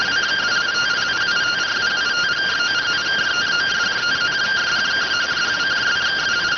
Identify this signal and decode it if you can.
Two-tone teleprinter signal, about 170 Hz shift.